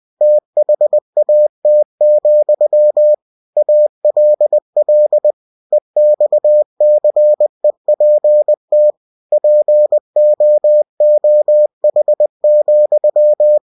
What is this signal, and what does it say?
Morse code, heard as a keyed tone, about 20 wpm, 600 Hz. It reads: THAT, ALL EXCEPT POOH,
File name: winnie-20wpm